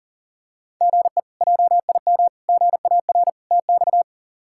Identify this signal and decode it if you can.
Morse code, heard as a keyed tone, about 37 wpm, 700 Hz. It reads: GE JIM GAR TX
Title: K1USN SST QSO Practice - 40wpm